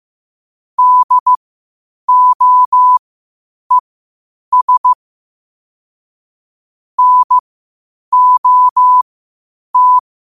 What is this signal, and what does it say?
Morse code, heard as a keyed tone, about 15 wpm, spaced as for 8 wpm, 1000 Hz. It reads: DOES NOT
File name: qotdfiles / 8wpm